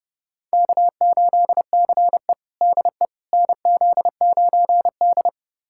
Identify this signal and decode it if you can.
Morse code, heard as a keyed tone, about 32 wpm, 700 Hz. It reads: K8CE DE NZ9D
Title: USA Callsign DE Callsign - 4 Characters - 30wpm